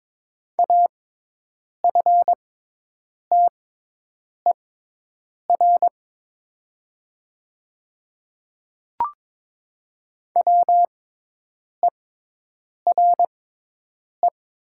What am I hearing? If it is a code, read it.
Morse code, heard as a keyed tone, about 22 wpm, spaced as for 8 wpm, 700 Hz. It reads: AFTER WERE
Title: 6x Character Spacing - Top 200 Words - 22wpm